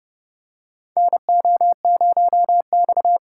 Morse code, heard as a keyed tone, about 30 wpm, 700 Hz. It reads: NO0X
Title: USA Call Signs - 4 Characters - 2x Character Spacing - 30wpm